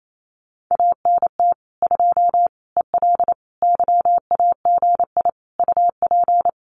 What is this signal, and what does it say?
Morse code, heard as a keyed tone, about 28 wpm, 700 Hz. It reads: ANT 2 EL YAGI UP
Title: Common QSO Phrases Encore 28wpm